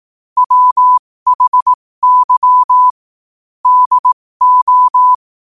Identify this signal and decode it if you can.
Morse code, heard as a keyed tone, about 18 wpm, spaced as for 15 wpm, 1000 Hz. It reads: WHY DO